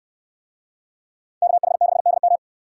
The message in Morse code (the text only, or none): KH6DM